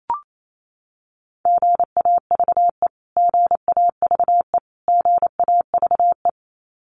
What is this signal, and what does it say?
Morse code, heard as a keyed tone, about 28 wpm, 700 Hz. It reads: GAVE GAVE GAVE
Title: Top 500 Words Repeated 28wpm